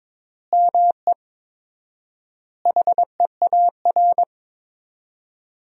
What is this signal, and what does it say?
Morse code, heard as a keyed tone, about 22 wpm, 700 Hz. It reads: ME HEAR
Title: 4x Word-Spacing - Sentences from Top 300 Words - 22wpm